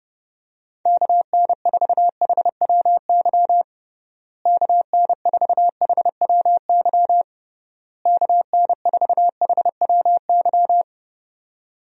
Morse code, heard as a keyed tone, about 31 wpm, 700 Hz. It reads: KN4HWY KN4HWY KN4HWY
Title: USA Call Signs - 6 Characters - Repeated 3x - 30wpm